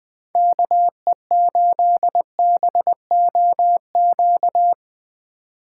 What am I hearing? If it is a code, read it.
Morse code, heard as a keyed tone, about 22 wpm, 700 Hz. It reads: KE8BOQ
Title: Speed Racing - USA 2x3 Call Signs 20wpm